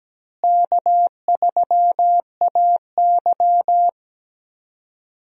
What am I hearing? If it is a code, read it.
Morse code, heard as a keyed tone, about 17 wpm, 700 Hz. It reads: K3AY
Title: USA Call Signs - 4 Characters - Repeated 2x - 17wpm